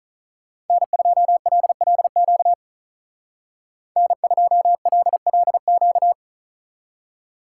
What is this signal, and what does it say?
Morse code, heard as a keyed tone, about 45 wpm, 700 Hz. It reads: N2LLQ N2LLQ